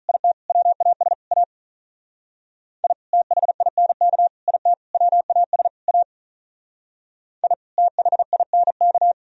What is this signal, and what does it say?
Morse code, heard as a keyed tone, about 44 wpm, 700 Hz. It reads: IT WAS A I THINK IT WAS A I THINK
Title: Speed-Racing - Sets of 5 Words 35wpm